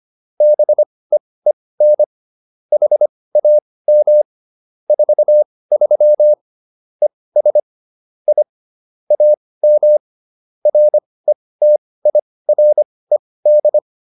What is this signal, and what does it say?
Morse code, heard as a keyed tone, about 25 wpm, spaced as for 18 wpm, 600 Hz. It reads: BEEN HAM 43 ES I AM RETIRED